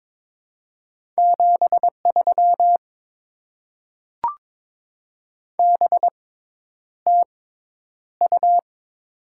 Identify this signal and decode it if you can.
Morse code, heard as a keyed tone, about 22 wpm, spaced as for 10 wpm, 700 Hz. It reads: 73 BTU
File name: Top 20 QSO Elements - 6x Character Spacing - 22wpm